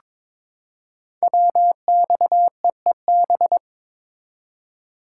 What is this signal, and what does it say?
Morse code, heard as a keyed tone, about 22 wpm, 700 Hz. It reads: WXEEB
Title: Code Groups - 5 Letters - 22wpm